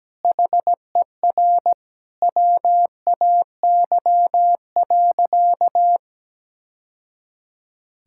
Morse code, heard as a keyed tone, about 17 wpm, 700 Hz. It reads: HER WAY.
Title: Sentences from Top 1000 Words - Review Part 5 - 17wpm